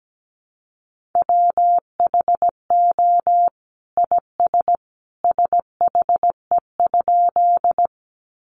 Morse code, heard as a keyed tone, about 17 wpm, 700 Hz. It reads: WHO IS SHE?
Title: Sentences from Top 100 Words 17wpm